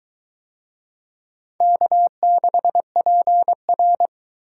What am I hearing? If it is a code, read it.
Morse code, heard as a keyed tone, about 20 wpm, 700 Hz. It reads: K6PR